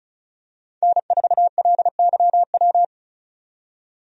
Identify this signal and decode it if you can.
Morse code, heard as a keyed tone, about 34 wpm, 700 Hz. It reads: N4LYW
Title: USA Call Signs - 5 Characters - 35wpm